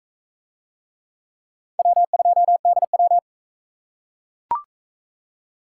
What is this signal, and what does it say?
Morse code, heard as a keyed tone, about 42 wpm, 700 Hz. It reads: W2DW